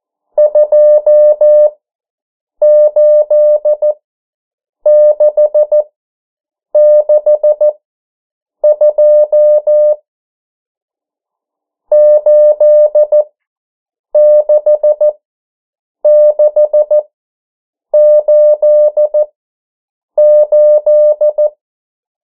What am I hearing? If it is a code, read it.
Morse code, heard as a keyed tone, about 14 wpm, spaced as for 7 wpm, 600 Hz. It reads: 28662 86688
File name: morse mp3 / 09 urok 8